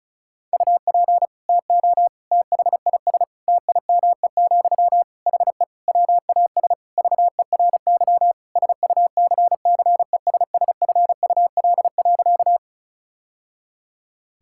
Morse code, heard as a keyed tone, about 36 wpm, 700 Hz. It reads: UP TO THIS TIME, HE WAS VERY SUCCESSFUL.
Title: Sentences from Top 1000 Words - Review Part 6 - 35wpm